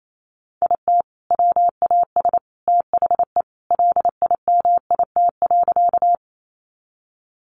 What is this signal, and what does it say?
Morse code, heard as a keyed tone, about 28 wpm, 700 Hz. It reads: IT WAS THE LIMIT.
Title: Sentences from Top 1500 Words 28wpm